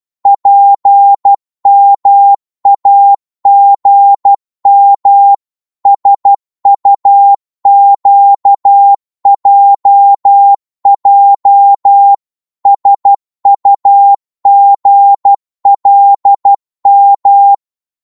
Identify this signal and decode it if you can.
Morse code, heard as a keyed tone, about 12 wpm, 800 Hz. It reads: PMAGM SUQJJ SUGLM
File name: Morse / ltr-12wpm